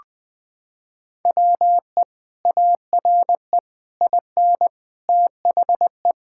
Morse code, heard as a keyed tone, about 21 wpm, 700 Hz. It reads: WE ARE IN THE
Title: Sentences with 6-Word Phrases - 20wpm